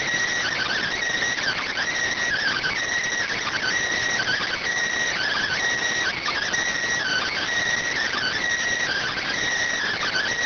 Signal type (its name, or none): Emergency Alert System (SAME) tones